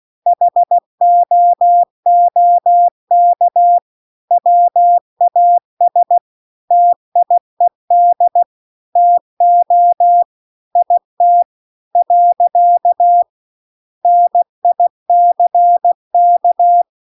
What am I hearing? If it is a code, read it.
Morse code, heard as a keyed tone, about 16 wpm, 700 Hz. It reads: HOOK WAS TIED TO IT . NICK